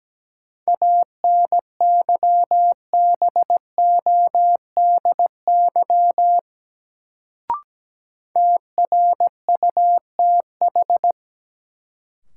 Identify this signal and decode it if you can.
Morse code, heard as a keyed tone, about 17 wpm, 700 Hz. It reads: ANYBODY TRUTH
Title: Sets of 5 Words - New Word Drill - 17wpm